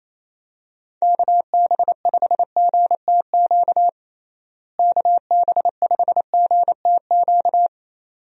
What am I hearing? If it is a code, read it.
Morse code, heard as a keyed tone, about 28 wpm, 700 Hz. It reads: KB5GTQ KB5GTQ
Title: USA Call Signs - 6 Characters - Repeated 2x - 28wpm